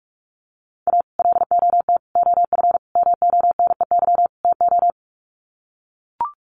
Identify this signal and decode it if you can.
Morse code, heard as a keyed tone, about 44 wpm, 700 Hz. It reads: A LOT OF MONEY TO
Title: Sets of 5 Words Encore2 45wpm